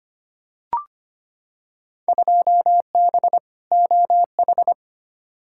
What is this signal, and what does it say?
Morse code, heard as a keyed tone, about 25 wpm, 700 Hz. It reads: 2B OH
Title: Field Day Exchanges - 25wpm